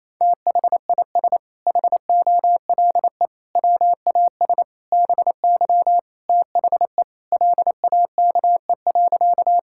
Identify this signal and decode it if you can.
Morse code, heard as a keyed tone, about 28 wpm, 700 Hz. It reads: THIS HOLE WAS BY THE LAKE.
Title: Sentences from Top 1600 Words 28wpm